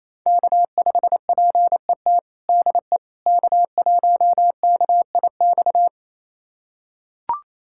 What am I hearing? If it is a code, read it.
Morse code, heard as a keyed tone, about 28 wpm, 700 Hz. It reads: K5PET DE K1KIX
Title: USA Callsign DE Callsign - 5 Characters - 28wpm